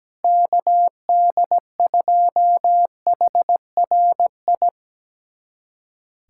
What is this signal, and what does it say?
Morse code, heard as a keyed tone, about 17 wpm, 700 Hz. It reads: KD2HRI
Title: USA Callsign DE Callsign 17wpm